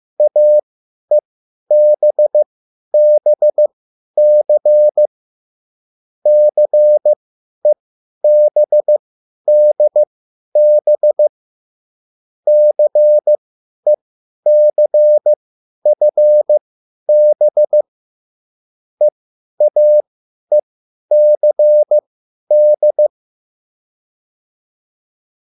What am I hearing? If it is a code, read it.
Morse code, heard as a keyed tone, about 15 wpm, spaced as for 10 wpm, 600 Hz. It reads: AEBBC CEBDB CECFB EAECD
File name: QSL / CW lesson